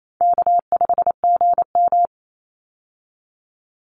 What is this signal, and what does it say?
Morse code, heard as a keyed tone, about 28 wpm, 700 Hz. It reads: K5GM
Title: Call Signs - 28wpm